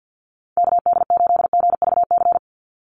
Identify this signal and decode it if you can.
Morse code, heard as a keyed tone, about 51 wpm, 700 Hz. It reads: KD8GVC
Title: USA Call Signs 50wpm